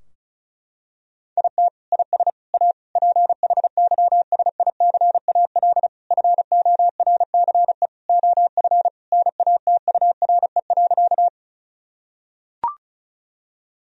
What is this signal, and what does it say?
Morse code, heard as a keyed tone, about 36 wpm, 700 Hz. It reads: IT IS A PHYSICAL FORCE OF NATURE.
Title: Sentences from Top 1000 Words - Review Part 4 - 35wpm